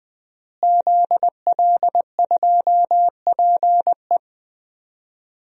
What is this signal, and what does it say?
Morse code, heard as a keyed tone, about 21 wpm, 700 Hz. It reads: ZL2PE
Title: DXCC-100 Call Signs - 5 Characters - Repeated 3x - 20wpm